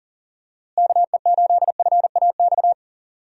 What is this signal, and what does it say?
Morse code, heard as a keyed tone, about 38 wpm, 700 Hz. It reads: KE8FAX
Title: USA Call Signs - 6 Characters - Repeated 3x - 40wpm